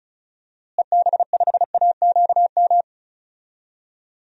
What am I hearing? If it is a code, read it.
Morse code, heard as a keyed tone, about 36 wpm, 700 Hz. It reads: EB5AQM
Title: DXCC-100 Call Signs - 6 Characters - Repeated 3x - 35wpm